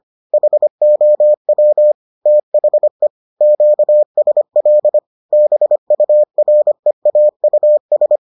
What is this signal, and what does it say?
Morse code, heard as a keyed tone, about 25 wpm, 600 Hz. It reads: HOW THE QSL BUREAUS